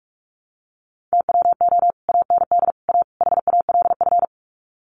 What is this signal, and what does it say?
Morse code, heard as a keyed tone, about 44 wpm, 700 Hz. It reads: TWO AND A HALF
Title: Sets of 4 Words 45wpm